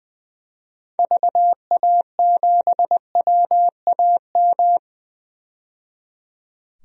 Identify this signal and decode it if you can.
Morse code, heard as a keyed tone, about 20 wpm, 700 Hz. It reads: VA7WAM
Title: DXCC-100 Call Signs - 6 Characters - 20wpm